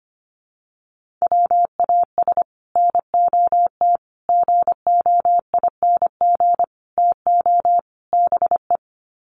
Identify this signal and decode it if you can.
Morse code, heard as a keyed tone, about 25 wpm, 700 Hz. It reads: WAS NOT GOING TO BE
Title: Sets of 5 Words Encore 25wpm